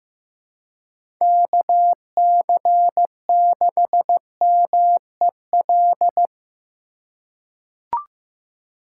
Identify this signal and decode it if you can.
Morse code, heard as a keyed tone, about 15 wpm, 700 Hz. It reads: KC6MEL
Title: USA Call Signs - 4x Character Spacing - 15wpm